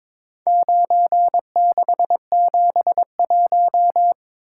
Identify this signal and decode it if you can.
Morse code, heard as a keyed tone, about 23 wpm, 700 Hz. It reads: 9671
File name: practice / 22wpm / Four Letter-Number Combinations - 22wpm